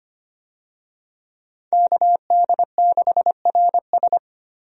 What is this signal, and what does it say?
Morse code, heard as a keyed tone, about 25 wpm, 700 Hz. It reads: KD6RS
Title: USA Call Signs - 5 Characters - Repeated 3x - 25wpm